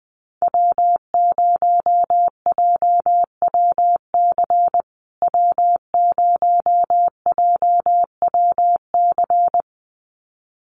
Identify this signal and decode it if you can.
Morse code, heard as a keyed tone, about 20 wpm, 700 Hz. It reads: W0JWC W0JWC
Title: USA Call Signs Repeated 2x 20wpm